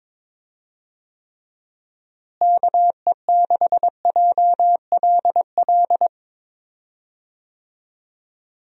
Morse code, heard as a keyed tone, about 22 wpm, 700 Hz. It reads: KE6JLL
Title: USA Call Signs - 6 Characters - Repeated 3x - 22wpm